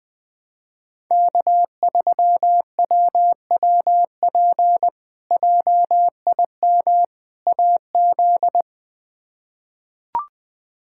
Morse code, heard as a keyed tone, about 21 wpm, 700 Hz. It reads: K3WWP JIM AZ
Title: K1USN SST QSO Practice - 20wpm